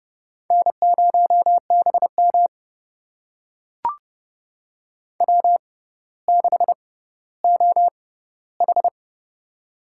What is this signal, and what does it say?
Morse code, heard as a keyed tone, about 32 wpm, spaced as for 16 wpm, 700 Hz. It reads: N0BM W6OH
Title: USA Call Signs - 4 Characters - 6x Character Spacing - 30wpm